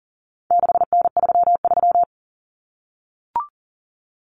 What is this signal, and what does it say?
Morse code, heard as a keyed tone, about 39 wpm, 700 Hz. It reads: 6N33
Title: Common Cut Numbers - 4-Digit 40wpm